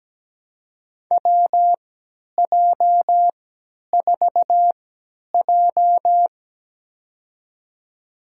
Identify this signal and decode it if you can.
Morse code, heard as a keyed tone, about 17 wpm, spaced as for 10 wpm, 700 Hz. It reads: WJ4J
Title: USA Call Signs - 4 Characters - 3x Character Spacing - 17wpm